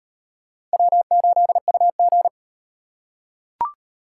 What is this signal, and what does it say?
Morse code, heard as a keyed tone, about 37 wpm, 700 Hz. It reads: W8UG